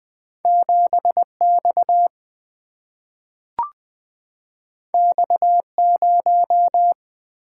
Morse code, heard as a keyed tone, about 20 wpm, 700 Hz. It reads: 7 X X 0
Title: Single-Letter Single-Number 20wpm